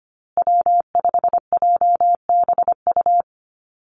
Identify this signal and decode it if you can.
Morse code, heard as a keyed tone, about 25 wpm, 700 Hz. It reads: W5JBU
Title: USA Call Signs Repeated 3x 25wpm